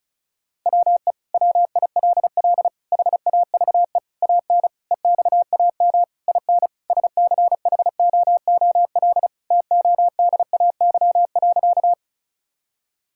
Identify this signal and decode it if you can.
Morse code, heard as a keyed tone, about 36 wpm, 700 Hz. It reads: WE WILL HAVE AN EXAM IN SCHOOL TODAY.
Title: Sentences from Top 1000 Words - Review Part 5 - 35wpm